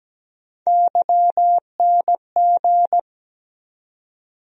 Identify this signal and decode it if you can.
Morse code, heard as a keyed tone, about 17 wpm, 700 Hz. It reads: YNG
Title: USA Call Sign Postfixes - 3 Letters 17wpm